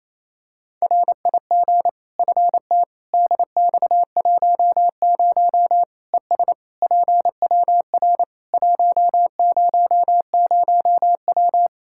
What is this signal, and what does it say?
Morse code, heard as a keyed tone, about 28 wpm, 700 Hz. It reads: RIG FT DX10 ES PWR 100W
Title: K1USN SST QSO Practice 28wpm